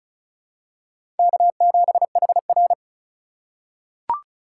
Morse code, heard as a keyed tone, about 34 wpm, 700 Hz. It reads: K7HR
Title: USA Call Signs - 4 Characters - 3x Character Spacing - 35wpm